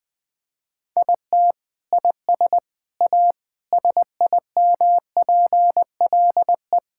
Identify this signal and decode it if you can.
Morse code, heard as a keyed tone, about 20 wpm, 700 Hz. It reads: IT IS A SIMPLE
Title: Sentences from Top 1000 Words - Review Part 6 - 20wpm